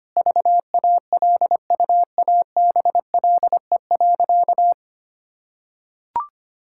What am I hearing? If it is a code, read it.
Morse code, heard as a keyed tone, about 25 wpm, 700 Hz. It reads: VALUABLE.
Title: Sentences from Top 1600 Words 25wpm